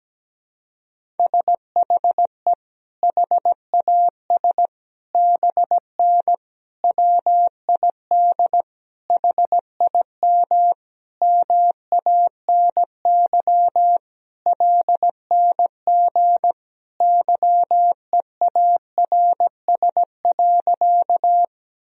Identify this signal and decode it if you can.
Morse code, heard as a keyed tone, about 17 wpm, 700 Hz. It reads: SHE HAS BN WID HIM MANY LNG YEARS.
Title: Sentences from Top 200 Words and Abbreviations 17wpm